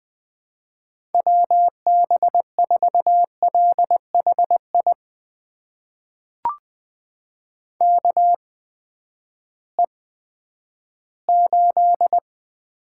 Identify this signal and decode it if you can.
Morse code, heard as a keyed tone, about 21 wpm, spaced as for 11 wpm, 700 Hz. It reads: WB4LHI KE8
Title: USA Call Signs - 6 Characters - 8x Character Spacing - 20wpm